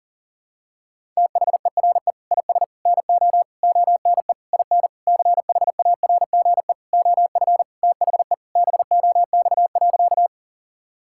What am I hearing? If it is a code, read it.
Morse code, heard as a keyed tone, about 40 wpm, 700 Hz. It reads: THERE IS NO ONE IN CHARGE OF THE BOX.
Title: Sentences from Top 1000 Words - Review Part 6 - 40wpm